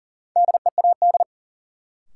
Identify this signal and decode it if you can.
Morse code, heard as a keyed tone, about 40 wpm, 700 Hz. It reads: DEAD
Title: Sentences from Top 1500 Words - New Word Drill - 40wpm